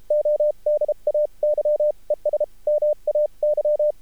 Morse code, heard as a keyed tone, about 33 wpm, spaced as for 29 wpm, 600 Hz. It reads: ODAY ES MAY
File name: archived / CRSnet